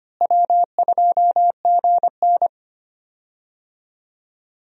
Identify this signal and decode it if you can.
Morse code, heard as a keyed tone, about 25 wpm, 700 Hz. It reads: W2GN
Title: USA Call Signs - 4 Characters - Repeated 2x - 25wpm